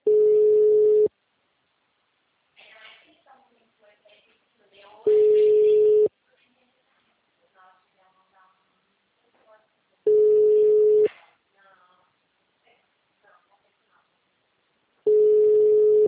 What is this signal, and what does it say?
Ringback tone (Europe, 425 Hz), with speech.